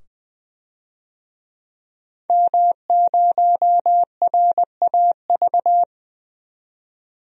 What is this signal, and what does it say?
Morse code, heard as a keyed tone, about 20 wpm, 700 Hz. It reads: M0RAV
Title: DXCC-100 Call Signs - 5 Characters - 20wpm